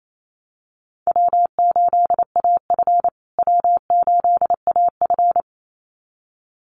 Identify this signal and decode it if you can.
Morse code, heard as a keyed tone, about 28 wpm, 700 Hz. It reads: W8AF W8AF
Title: USA Call Signs Repeated 2x 28wpm